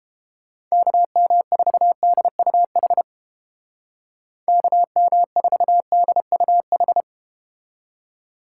Morse code, heard as a keyed tone, about 31 wpm, 700 Hz. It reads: KM4DUH KM4DUH